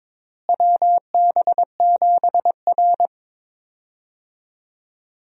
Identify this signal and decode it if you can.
Morse code, heard as a keyed tone, about 22 wpm, 700 Hz. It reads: WB7R
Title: USA Call Signs - 4 Characters - Repeated 2x - 22wpm